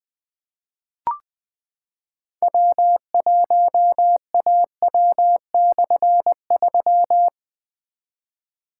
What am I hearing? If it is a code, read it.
Morse code, heard as a keyed tone, about 20 wpm, 700 Hz. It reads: W1AW/3
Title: Difficult Call Signs 20wpm